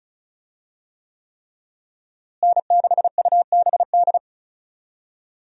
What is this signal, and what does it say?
Morse code, heard as a keyed tone, about 36 wpm, 700 Hz. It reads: N6UBD
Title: USA Call Signs - 5 Characters - Repeated 3x - 35wpm